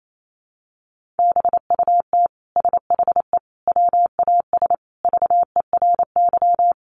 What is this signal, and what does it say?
Morse code, heard as a keyed tone, about 28 wpm, 700 Hz. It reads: BUT SHE WAS VERY
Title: Sentences from Top 1000 Words 28wpm